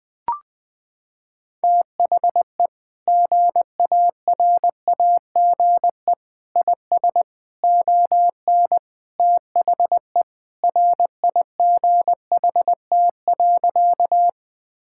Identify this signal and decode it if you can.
Morse code, heard as a keyed tone, about 20 wpm, 700 Hz. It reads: THE GARAGE IS ON THE RIGHT.
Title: Sentences from Top 1600 Words 20wpm